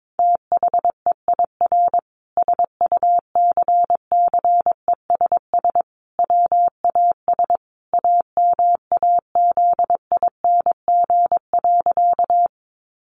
Morse code, heard as a keyed tone, about 22 wpm, 700 Hz. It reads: THEIR SUCCESS WAS AMAZING.
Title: Sentences from Top 1500 Words 22wpm